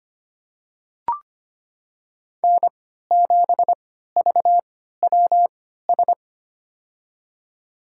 Morse code, heard as a keyed tone, about 25 wpm, spaced as for 14 wpm, 700 Hz. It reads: N7VWS
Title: USA Call Signs - 5 Characters - 3x Character Spacing - 25wpm